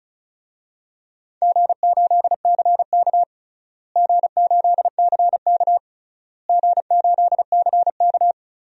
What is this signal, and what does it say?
Morse code, heard as a keyed tone, about 36 wpm, 700 Hz. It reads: G8CK G8CK G8CK
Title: DXCC-100 Call Signs - 4 Characters - Repeated 3x - 35wpm